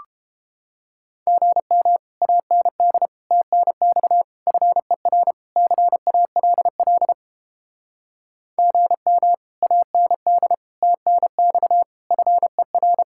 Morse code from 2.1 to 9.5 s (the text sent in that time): AND TNX FER CALL GM